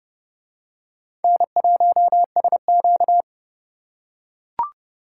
Morse code, heard as a keyed tone, about 29 wpm, 700 Hz. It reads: N1SQ